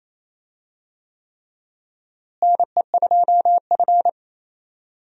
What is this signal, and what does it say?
Morse code, heard as a keyed tone, about 28 wpm, 700 Hz. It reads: NE2F
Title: USA Call Signs - 4 Characters - Repeated 3x - 28wpm